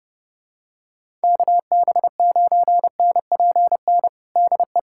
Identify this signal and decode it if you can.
Morse code, heard as a keyed tone, about 30 wpm, 700 Hz. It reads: KB9NPN DE
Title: USA Callsign DE Callsign 30wpm